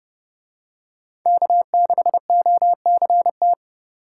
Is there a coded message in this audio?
Yes: Morse code from 1.3 to 3.5 s, heard as a keyed tone, about 29 wpm, 700 Hz. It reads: K6OCT